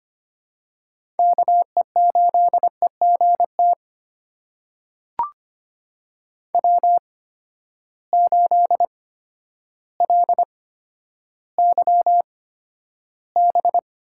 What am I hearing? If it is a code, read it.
Morse code, heard as a keyed tone, about 25 wpm, spaced as for 11 wpm, 700 Hz. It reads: KE8EGT W8LYB
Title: USA Call Signs - 8x Character Spacing - 25wpm